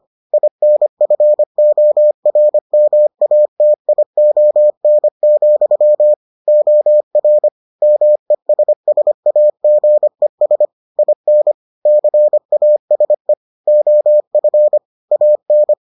Morse code, heard as a keyed tone, about 25 wpm, 600 Hz. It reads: INFORMATION, OR MESSAGES IN CASE OF AN